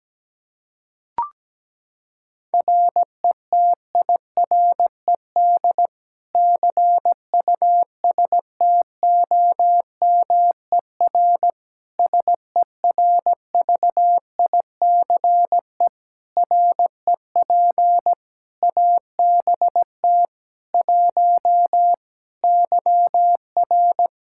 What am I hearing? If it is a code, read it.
Morse code, heard as a keyed tone, about 17 wpm, 700 Hz. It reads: RETIRED CUSTOMER SERVICE REP ABT 1 YR
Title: Practice QSO - Exchange 3 - More Info - 17wpm